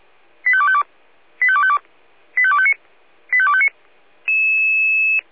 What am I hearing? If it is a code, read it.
Five-tone selective call (ZVEI), heard as quick runs of five tones, tones of about 70 ms. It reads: 85252 85252 85259 85259